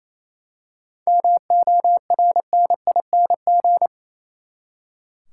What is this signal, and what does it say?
Morse code, heard as a keyed tone, about 28 wpm, 700 Hz. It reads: MORNING
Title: Seven Letter Words - 28wpm